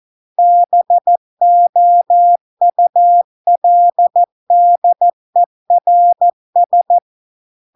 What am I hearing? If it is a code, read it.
Morse code, heard as a keyed tone, about 14 wpm, 700 Hz. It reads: BOULDERS